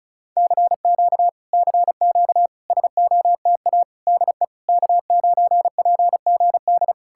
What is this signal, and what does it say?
Morse code, heard as a keyed tone, about 40 wpm, 700 Hz. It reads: CQ CQ SOTA DE K9PGD